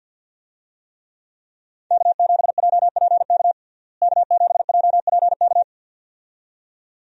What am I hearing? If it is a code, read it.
Morse code, heard as a keyed tone, about 49 wpm, 700 Hz. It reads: K7JPK K7JPK
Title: USA Call Signs - 5 Characters - Repeated 2x - 50wpm